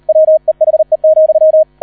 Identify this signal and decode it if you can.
Morse code, heard as a keyed tone, about 38 wpm, 630 Hz. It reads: WEHE,